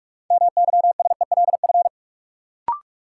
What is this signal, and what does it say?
Morse code, heard as a keyed tone, about 46 wpm, 700 Hz. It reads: MYSELF
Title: Sentences from Top 1300 Words - New Word Drill - 45wpm